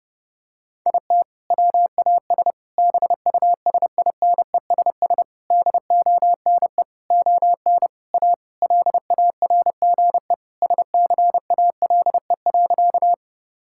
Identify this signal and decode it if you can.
Morse code, heard as a keyed tone, about 30 wpm, 700 Hz. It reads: IT WAS BUSINESS DONE ON A LARGE SCALE.
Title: Sentences from Top 1000 Words - Review Part 5 - 30wpm